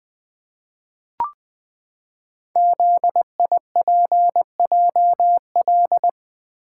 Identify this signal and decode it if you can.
Morse code, heard as a keyed tone, about 21 wpm, 700 Hz. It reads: ZIPJL
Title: Code Groups - 5 Letter:Numbers - 20wpm